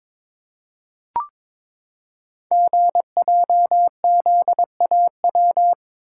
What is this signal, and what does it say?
Morse code, heard as a keyed tone, about 22 wpm, 700 Hz. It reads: GJZAW
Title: Code Groups - 5 Letters - 22wpm